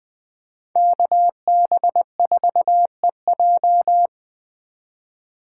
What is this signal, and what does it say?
Morse code, heard as a keyed tone, about 20 wpm, 700 Hz. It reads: KB4EJ
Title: Speed Racing - USA 2x2 Call Signs 17wpm